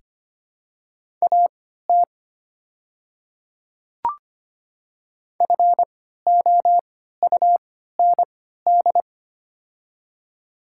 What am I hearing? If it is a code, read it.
Morse code, heard as a keyed tone, about 25 wpm, spaced as for 14 wpm, 700 Hz. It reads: AT FOUND
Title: 3x Character Spacing - Top 300 Words - 25wpm